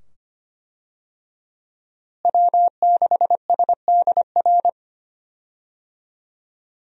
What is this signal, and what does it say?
Morse code, heard as a keyed tone, about 25 wpm, 700 Hz. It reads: W6SDR
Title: USA Call Signs - 5 Characters - Repeated 2x - 25wpm